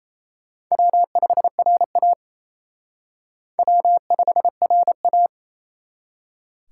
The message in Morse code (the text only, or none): W5RA W5RA